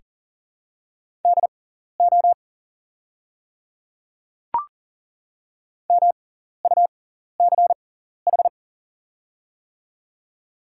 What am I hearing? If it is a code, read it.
Morse code, heard as a keyed tone, about 40 wpm, spaced as for 14 wpm, 700 Hz. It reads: DO MUCH
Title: 6x Character Spacing - Top 200 Words - 40wpm